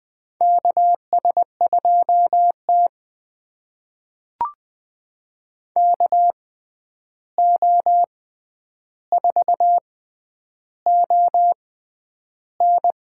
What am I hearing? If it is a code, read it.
Morse code, heard as a keyed tone, about 21 wpm, spaced as for 10 wpm, 700 Hz. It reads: KS2T KO4ON
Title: USA Call Signs - 6x Character Spacing - 20wpm